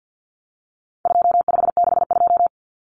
Morse code, heard as a keyed tone, about 50 wpm, 700 Hz. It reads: 2562
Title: Common Cut Numbers - 4-Digit 50wpm